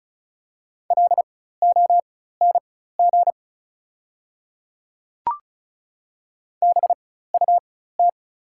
Morse code, heard as a keyed tone, about 34 wpm, spaced as for 16 wpm, 700 Hz. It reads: LONG BUT
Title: 4x Character Spacing - Top 500 Words - 35wpm